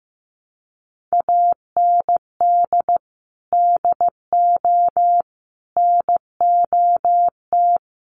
Morse code, heard as a keyed tone, about 15 wpm, 700 Hz. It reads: AND DO NOT
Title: Sentences from Top 1200 Words 15wpm